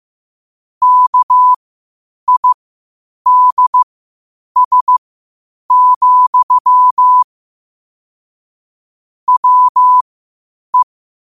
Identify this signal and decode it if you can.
Morse code, heard as a keyed tone, about 15 wpm, spaced as for 8 wpm, 1000 Hz. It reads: KIDS, WE
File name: qotdfiles / 8wpm